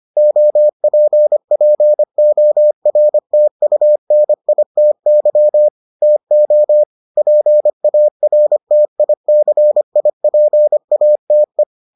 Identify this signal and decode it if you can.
Morse code, heard as a keyed tone, about 25 wpm, 600 Hz. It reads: OPPORTUNITY TO PARTICIPATE